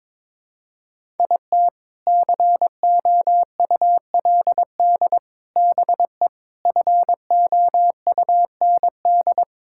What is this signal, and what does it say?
Morse code, heard as a keyed tone, about 21 wpm, 700 Hz. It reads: IT COULD BE FOUND
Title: Sentences from Top 1000 Words - Review Part 5 - 22wpm